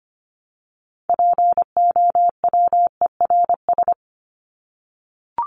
Morse code, heard as a keyed tone, about 25 wpm, 700 Hz. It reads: POWERS
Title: ES:S Suffix - Top 100 Words with Base Words 25wpm